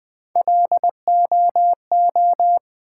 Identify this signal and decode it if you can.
Morse code, heard as a keyed tone, about 20 wpm, 700 Hz. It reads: LOO
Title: Build A Word - 1 Letter At A Time - 4-Letter Words - 20wpm